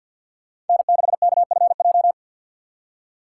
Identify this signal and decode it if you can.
Morse code, heard as a keyed tone, about 49 wpm, 700 Hz. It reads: N6KFJ
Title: USA Call Signs - 5 Characters - 3x Character Spacing - 50wpm